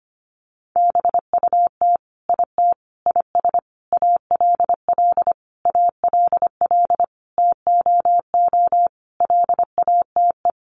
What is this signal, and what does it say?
Morse code, heard as a keyed tone, about 25 wpm, 700 Hz. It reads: BUT IT IS ALL ALL TOO LATE
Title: Sentences from Top 300 Words 25wpm